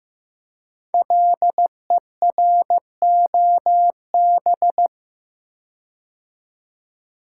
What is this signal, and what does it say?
Morse code, heard as a keyed tone, about 15 wpm, 700 Hz. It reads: LEROB
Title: Code Groups - 5 Letters - 15wpm